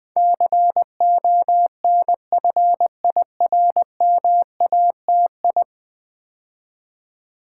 Morse code, heard as a keyed tone, about 20 wpm, 700 Hz. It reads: CONFIRMATI
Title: Build A Word - 5 Letters At A Time - 12-Letter Words - 20wpm